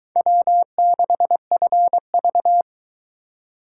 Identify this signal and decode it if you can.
Morse code, heard as a keyed tone, about 22 wpm, 700 Hz. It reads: W6FV